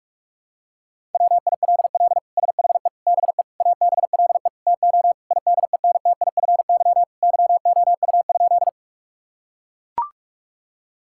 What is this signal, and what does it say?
Morse code, heard as a keyed tone, about 41 wpm, 700 Hz. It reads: WILL SHE BE ABLE TO IDENTIFY YOU?
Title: Sentences from Top 1000 Words - Review Part 6 - 45wpm